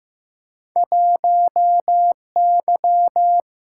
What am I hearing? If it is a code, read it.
Morse code, heard as a keyed tone, about 15 wpm, 700 Hz. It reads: 1Y
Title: Single-Letter Single-Number 15wpm